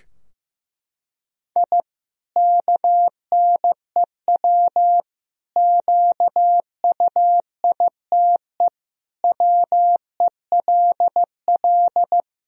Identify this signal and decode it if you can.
Morse code, heard as a keyed tone, about 15 wpm, 700 Hz. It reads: I KNEW QUITE WELL
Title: Sentences from Top 1000 Words - Review Part 3 - 15wpm